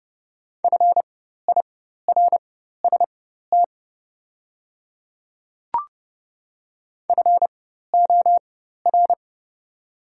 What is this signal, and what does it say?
Morse code, heard as a keyed tone, about 30 wpm, spaced as for 14 wpm, 700 Hz. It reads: FIRST FOR
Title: 4x Character Spacing - Top 100 Words - 30wpm